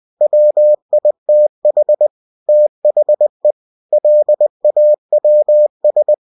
Morse code, heard as a keyed tone, about 20 wpm, 600 Hz. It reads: WITH THE LAWS